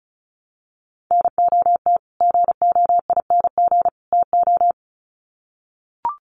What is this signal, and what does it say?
Morse code, heard as a keyed tone, about 34 wpm, 700 Hz. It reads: NOT GOING TO
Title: Sets of 3 Words 35wpm